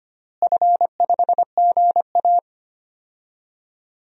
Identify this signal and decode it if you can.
Morse code, heard as a keyed tone, about 25 wpm, 700 Hz. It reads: F5GA
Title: DXCC-100 Call Signs - 4 Characters - Repeated 3x - 25wpm